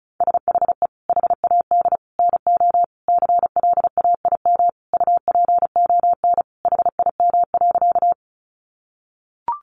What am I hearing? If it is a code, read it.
Morse code, heard as a keyed tone, about 34 wpm, 700 Hz. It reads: SHE HAD NO CLAIM UPON HIM.
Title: Sentences from Top 1500 Words 35wpm